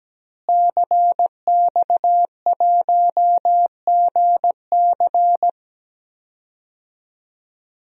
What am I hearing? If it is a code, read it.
Morse code, heard as a keyed tone, about 17 wpm, 700 Hz. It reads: CX1GC
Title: DXCC-100 Call Signs - 5 Characters - Repeated 3x - 17wpm